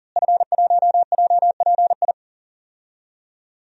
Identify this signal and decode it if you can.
Morse code, heard as a keyed tone, about 39 wpm, 700 Hz. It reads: F1JPI
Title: DXCC-100 Call Signs - 5 Characters - Repeated 2x - 40wpm